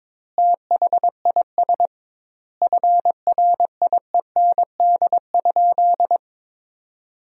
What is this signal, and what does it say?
Morse code, heard as a keyed tone, about 22 wpm, 700 Hz. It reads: THIS FRIEND?
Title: 2x Word-Spacing - Sentences from Top 300 Words - 22wpm